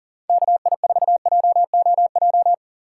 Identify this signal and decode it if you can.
Morse code, heard as a keyed tone, about 39 wpm, 700 Hz. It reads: KI4JOJ
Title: USA Call Signs - 6 Characters - 3x Character Spacing - 40wpm